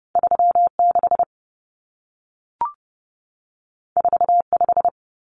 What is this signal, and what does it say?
Morse code, heard as a keyed tone, about 30 wpm, 700 Hz. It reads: 36 45
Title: Numbers - 2-Digit 30wpm